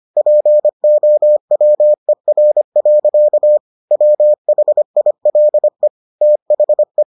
Morse code, heard as a keyed tone, about 25 wpm, 600 Hz. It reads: POWER. WHILE THE